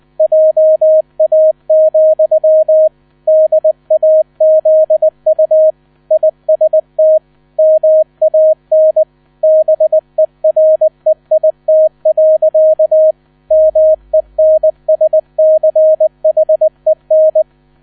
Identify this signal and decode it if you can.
Morse code, heard as a keyed tone, about 20 wpm, 630 Hz. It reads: JA, DAZU IST MAN BEREIT. MENSCHEN
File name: 000 Vorwort 20wpm